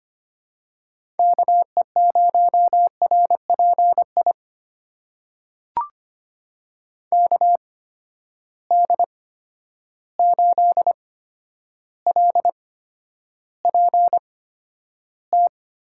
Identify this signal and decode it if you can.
Morse code, heard as a keyed tone, about 25 wpm, spaced as for 11 wpm, 700 Hz. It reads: KE0RPI KD8LPT
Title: USA Call Signs - 6 Characters - 8x Character Spacing - 25wpm